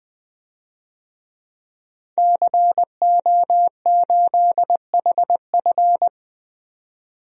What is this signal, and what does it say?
Morse code, heard as a keyed tone, about 20 wpm, 700 Hz. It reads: CO8HF
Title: DXCC-100 Call Signs - 5 Characters - 20wpm